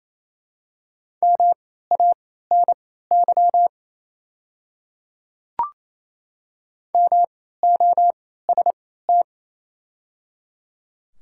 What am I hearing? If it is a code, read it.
Morse code, heard as a keyed tone, about 27 wpm, spaced as for 16 wpm, 700 Hz. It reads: MANY MOST
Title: 3x Character Spacing - Top 100 Words - 28wpm